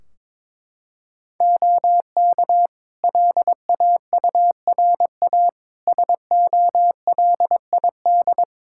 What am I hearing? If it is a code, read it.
Morse code, heard as a keyed tone, about 22 wpm, 700 Hz. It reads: OK LAURA SOLID
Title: K1USN SST QSO Practice 22wpm